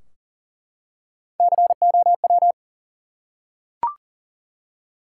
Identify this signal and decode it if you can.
Morse code, heard as a keyed tone, about 38 wpm, 700 Hz. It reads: COW
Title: Three Letter Words - 40wpm